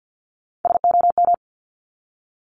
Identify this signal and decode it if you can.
Morse code, heard as a keyed tone, about 50 wpm, 700 Hz. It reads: SOM
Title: Top Three Letter Combinations in Words 50wpm